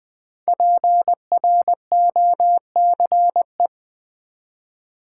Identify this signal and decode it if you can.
Morse code, heard as a keyed tone, about 20 wpm, 700 Hz. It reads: PROCE
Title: Build A Word - 5 Letters At A Time - 10-Letter Words - 20wpm